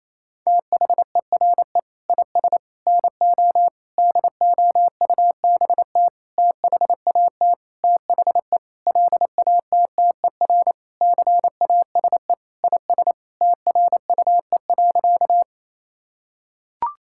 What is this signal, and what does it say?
Morse code, heard as a keyed tone, about 28 wpm, 700 Hz. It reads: THERE IS NO DOUBT THAT THE LATTER CASE IS TRUE.
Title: Sentences from Top 1600 Words 28wpm